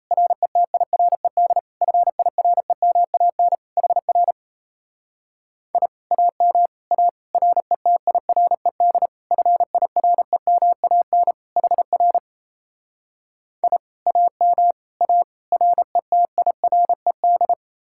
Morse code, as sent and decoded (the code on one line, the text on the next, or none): .-. - .. .-. -.. / ..-. .. .-. -- .- -. / .... .-. / .. / .- -- / .- / .-. - .. .-. -.. / ..-. .. .-. -- .- -. / .... .-. / .. / .- -- / .- / .-. - .. .-. -..
RETIRED FIREMAN HR I AM A RETIRED FIREMAN HR I AM A RETIRED